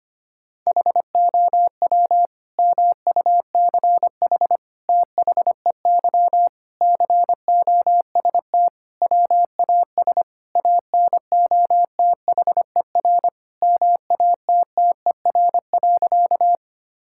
Morse code, heard as a keyed tone, about 25 wpm, 700 Hz. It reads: HOW MUCH THEY COST WAS ANOTHER MATTER.
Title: Sentences from Top 1000 Words - Review Part 5 - 25wpm